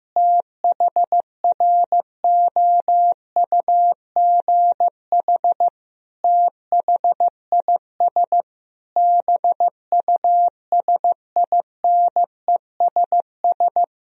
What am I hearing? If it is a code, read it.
Morse code, heard as a keyed tone, about 15 wpm, 700 Hz. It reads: THROUGH THIS BUSINESS
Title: Sentences from Top 1000 Words - Review Part 2 - 15wpm